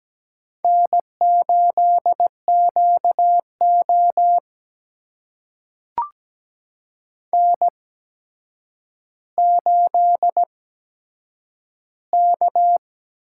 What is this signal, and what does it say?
Morse code, heard as a keyed tone, about 17 wpm, spaced as for 8 wpm, 700 Hz. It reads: N8QO N8K
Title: USA Call Signs - 4 Characters - 8x Character Spacing - 17wpm